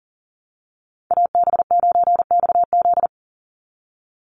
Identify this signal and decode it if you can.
Morse code, heard as a keyed tone, about 40 wpm, 700 Hz. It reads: AB9XZ
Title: USA Call Signs 40wpm